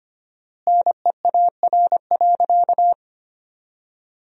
Morse code, heard as a keyed tone, about 25 wpm, 700 Hz. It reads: NEAR.
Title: 4x Word-Spacing - Sentences from Top 300 Words - 25wpm